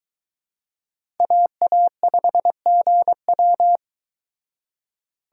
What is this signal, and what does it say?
Morse code, heard as a keyed tone, about 23 wpm, 700 Hz. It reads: AA5GW